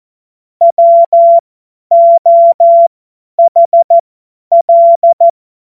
Morse code, heard as a keyed tone, about 14 wpm, spaced as for 10 wpm, 680 Hz. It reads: WOHL
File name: Morsekurs / Witze / Witz 10. Mai 2022 50 F70